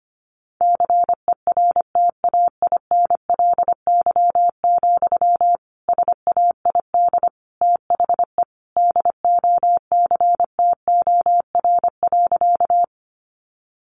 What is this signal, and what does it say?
Morse code, heard as a keyed tone, about 25 wpm, 700 Hz. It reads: CERTAINLY, SAID THE DOCTOR.
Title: Sentences from Top 1500 Words 25wpm